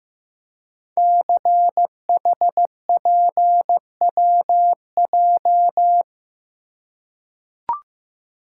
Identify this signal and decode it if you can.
Morse code, heard as a keyed tone, about 15 wpm, 700 Hz. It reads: CHPWJ
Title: Code Groups - 5 Letters - 15wpm